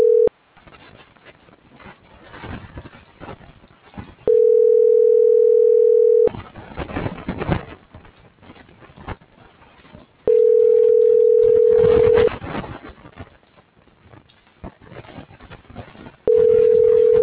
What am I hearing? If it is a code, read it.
Ringback tone (North America), 440 + 480 Hz.